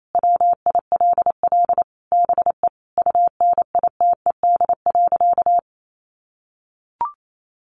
Morse code, heard as a keyed tone, about 28 wpm, 700 Hz. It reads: WILL BE UNITED.
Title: Sentences from Top 600 Words 28wpm